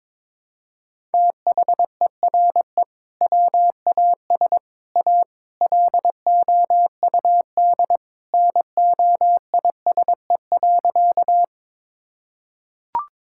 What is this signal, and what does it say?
Morse code, heard as a keyed tone, about 21 wpm, 700 Hz. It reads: THERE WAS A LOUD NOISE.
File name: practice / 22wpm / Sentences with 3-Word Phrases - 22wpm